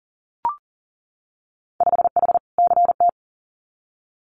Morse code, heard as a keyed tone, about 40 wpm, 700 Hz. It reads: 5H CT
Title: Winter Field Day Exchanges Encore 40wpm